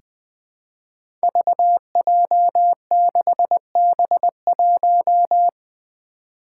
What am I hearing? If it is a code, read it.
Morse code, heard as a keyed tone, about 20 wpm, 700 Hz. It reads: VJ6B1
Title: Five Challenging Letter-Number Combinations 20wpm